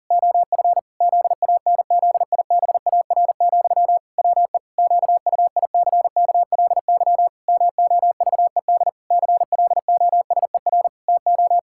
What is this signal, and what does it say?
Morse code, heard as a keyed tone, about 40 wpm, 700 Hz. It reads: OF ZANZIBAR, WE QUICKLY MOVED CLOSER TO
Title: Pangram Challenge v2 - 40wpm